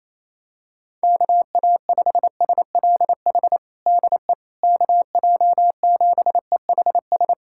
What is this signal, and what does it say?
Morse code, heard as a keyed tone, about 28 wpm, 700 Hz. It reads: KA5SLH DE KJ7EHS
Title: USA Callsign DE Callsign 28wpm